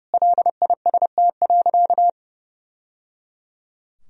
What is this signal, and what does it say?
Morse code, heard as a keyed tone, about 30 wpm, 700 Hz. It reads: LIST.
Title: Sentences from Top 1000 Words - Review Part 2 - 30wpm